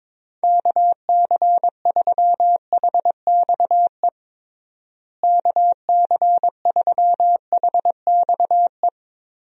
Morse code, heard as a keyed tone, about 22 wpm, 700 Hz. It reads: KC3HXE KC3HXE
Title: USA Call Signs - 6 Characters - Repeated 3x - 22wpm